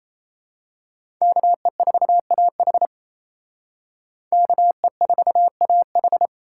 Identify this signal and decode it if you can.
Morse code, heard as a keyed tone, about 32 wpm, 700 Hz. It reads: KE4AH KE4AH
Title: Speed Racing - USA 2x2 Call Signs 28wpm